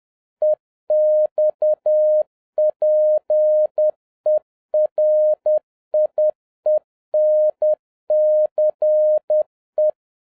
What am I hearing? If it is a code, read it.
Morse code, heard as a keyed tone, about 10 wpm, 610 Hz. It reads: EXPERIENCE